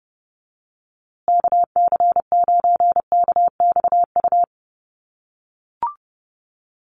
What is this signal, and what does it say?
Morse code, heard as a keyed tone, about 30 wpm, 700 Hz. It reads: KC9KXU
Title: USA Call Signs Repeated 3x 30wpm